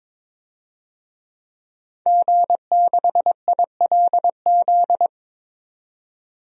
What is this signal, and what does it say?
Morse code, heard as a keyed tone, about 22 wpm, 700 Hz. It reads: G6ILZ
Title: DXCC-100 Call Signs - 5 Characters - 22wpm